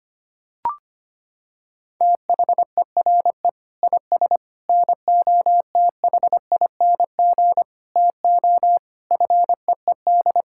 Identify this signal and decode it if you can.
Morse code, heard as a keyed tone, about 25 wpm, 700 Hz. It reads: THERE IS NOTHING TO FEED
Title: Sentences from Top 1600 Words 25wpm